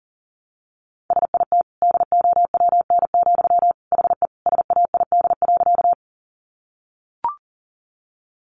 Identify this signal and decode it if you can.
Morse code, heard as a keyed tone, about 40 wpm, 700 Hz. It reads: SIT DOWN, HE SAID.
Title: Sentences from Top 500 Words 40wpm